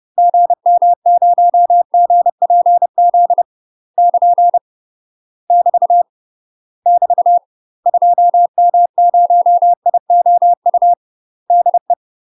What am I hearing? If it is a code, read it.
Morse code, heard as a keyed tone, about 30 wpm, 700 Hz. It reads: GM0GPZ <KN> <BT> <BT> 2M0IOU DE
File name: UK MASTER QSOs TRAINING / 30WPM / UK MASTER QSOs 0038